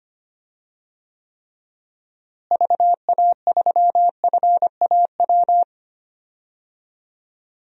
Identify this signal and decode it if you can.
Morse code, heard as a keyed tone, about 25 wpm, 700 Hz. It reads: VA3FAW
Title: DXCC-100 Call Signs - 6 Characters - Repeated 3x - 25wpm